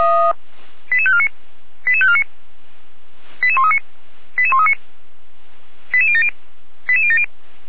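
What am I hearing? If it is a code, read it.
Five-tone selective call (ZVEI), heard as quick runs of five tones, tones of about 80 ms. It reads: 88549 88549 88139 88139 88070 88070
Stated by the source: Five-tone ZVEI selective call 88012, received on SDR by feuerwehr-orlamuende.org fire brigade server.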